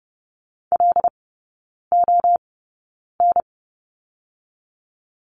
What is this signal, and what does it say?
Morse code, heard as a keyed tone, about 30 wpm, 700 Hz. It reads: L O N
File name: Lesson 19e - Letter w - Three Characters 30wpm